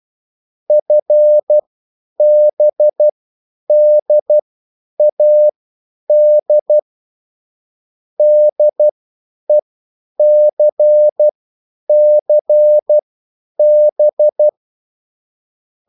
Morse code, heard as a keyed tone, about 12 wpm, spaced as for 9 wpm, 600 Hz. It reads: FBDAD DECCB
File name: QSL / CW lesson